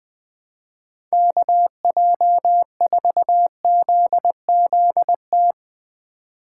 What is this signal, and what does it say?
Morse code, heard as a keyed tone, about 20 wpm, 700 Hz. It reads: KJ4ZZT
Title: USA Call Signs - 6 Characters - 4x Character Spacing - 20wpm